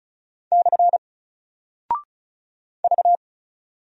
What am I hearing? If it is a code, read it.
Morse code, heard as a keyed tone, about 36 wpm, 700 Hz. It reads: / V
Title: Single Character (Letter:Number:Punctuation:Prosign) 35wpm